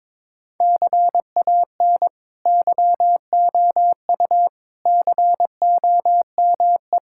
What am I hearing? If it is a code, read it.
Morse code, heard as a keyed tone, about 22 wpm, 700 Hz. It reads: CAN YOU COME
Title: Sentences from Top 1000 Words - Review Part 5 - 22wpm